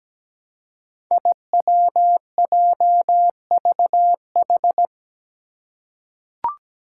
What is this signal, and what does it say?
Morse code, heard as a keyed tone, about 17 wpm, 700 Hz. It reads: IWJVH
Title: Five Challenging Letter-Number Combinations 17wpm